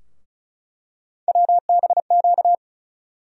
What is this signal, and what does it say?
Morse code, heard as a keyed tone, about 34 wpm, 700 Hz. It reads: WBQ
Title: USA Call Sign Postfixes - 3 Letters 35wpm